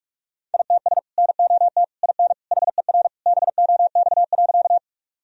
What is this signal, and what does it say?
Morse code, heard as a keyed tone, about 46 wpm, 700 Hz. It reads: ITS NOT IN HER BOX.
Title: Sentences from Top 1000 Words - Review Part 2 - 45wpm